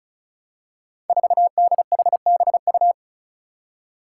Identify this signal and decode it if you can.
Morse code, heard as a keyed tone, about 36 wpm, 700 Hz. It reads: 4DHBU
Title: Five Challenging Letter-Number Combinations 35wpm